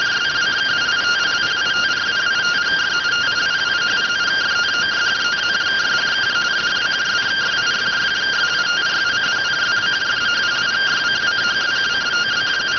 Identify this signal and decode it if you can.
Two-tone teleprinter signal, about 170 Hz shift.